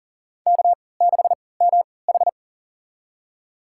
Morse code, heard as a keyed tone, about 40 wpm, spaced as for 23 wpm, 700 Hz. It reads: K6MH
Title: USA Call Signs - 4 Characters - 3x Character Spacing - 40wpm